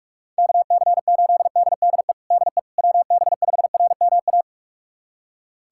Morse code, heard as a keyed tone, about 46 wpm, 700 Hz. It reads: KC8DDE DE WB5RMA
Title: USA Callsign DE Callsign - 6 Characters - 45wpm